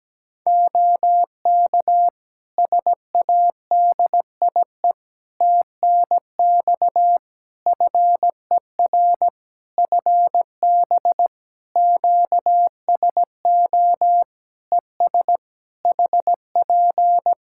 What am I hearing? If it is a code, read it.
Morse code, heard as a keyed tone, about 17 wpm, 700 Hz. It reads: OK SADIE TNX FER FB QSO ES HP
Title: Practice QSO - Exchange 4 - Ending - 17wpm